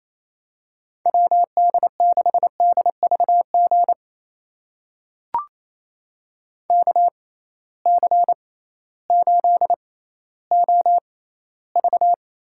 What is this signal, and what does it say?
Morse code, heard as a keyed tone, about 28 wpm, spaced as for 15 wpm, 700 Hz. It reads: WD6DVG KC8OV
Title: USA Call Signs - 6 Characters - 6x Character Spacing - 28wpm